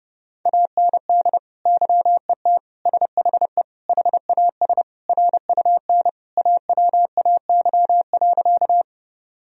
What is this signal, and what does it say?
Morse code, heard as a keyed tone, about 30 wpm, 700 Hz. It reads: AND YET SHE HAS RUN AWAY.
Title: Sentences from Top 1000 Words - Review Part 1 - 30wpm